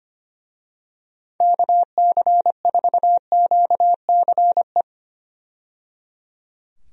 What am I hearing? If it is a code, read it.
Morse code, heard as a keyed tone, about 25 wpm, 700 Hz. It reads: KC4QCE
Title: USA Call Signs - 6 Characters - 25wpm